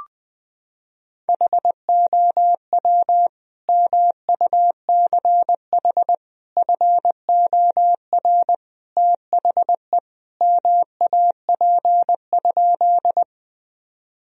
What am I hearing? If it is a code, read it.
Morse code, heard as a keyed tone, about 20 wpm, 700 Hz. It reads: HOW MUCH FOR THE MAP?
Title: Sentences from Top 1000 Words - Review Part 2 - 20wpm